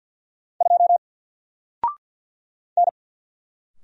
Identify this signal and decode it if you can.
Morse code, heard as a keyed tone, about 50 wpm, 700 Hz. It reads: 2 N
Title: Single Letter-Number 50wpm